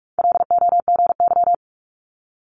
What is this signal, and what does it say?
Morse code, heard as a keyed tone, about 46 wpm, 700 Hz. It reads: LOGY
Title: Multi-letter Phonemes Quick Fire 45wpm